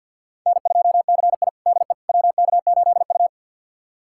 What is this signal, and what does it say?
Morse code, heard as a keyed tone, about 50 wpm, 700 Hz. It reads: N2CI DE WK8U
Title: USA Callsign DE Callsign - 4 Characters - 50wpm